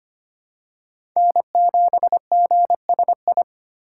Morse code, heard as a keyed tone, about 25 wpm, 700 Hz. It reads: N7GSI
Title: USA Call Signs - 2x Character Spacing - 25wpm